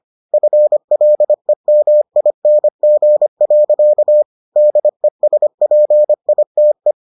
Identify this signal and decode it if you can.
Morse code, heard as a keyed tone, about 25 wpm, 600 Hz. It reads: FLEMING. DESPITE